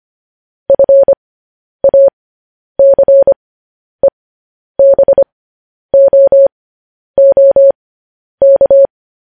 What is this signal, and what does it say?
Morse code, heard as a keyed tone, about 25 wpm, spaced as for 10 wpm, 550 Hz. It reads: FACEBOOK
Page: News Headlines in Morse Code at 10 WPM